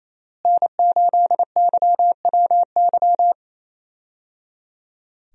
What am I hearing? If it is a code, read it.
Morse code, heard as a keyed tone, about 27 wpm, 700 Hz. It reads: N8YWY
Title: USA Call Signs - 5 Characters - Repeated 3x - 28wpm